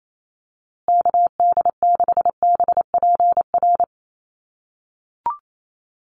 Morse code, heard as a keyed tone, about 28 wpm, 700 Hz. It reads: KD6BPR
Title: USA Call Signs Repeated 2x 28wpm